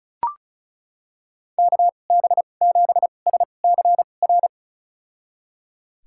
Morse code, heard as a keyed tone, about 36 wpm, spaced as for 25 wpm, 700 Hz. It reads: KB7SCR
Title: USA Call Signs - 6 Characters - 2x Character Spacing - 35wpm